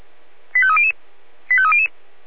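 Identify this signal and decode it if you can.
Five-tone selective call (ZVEI), heard as quick runs of five tones, tones of about 70 ms. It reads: 85300 85300
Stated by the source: ZVEI five-tone selective call 85849, received by SDR.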